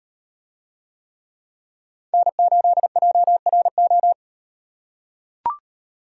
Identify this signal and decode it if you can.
Morse code, heard as a keyed tone, about 38 wpm, 700 Hz. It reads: N8JRO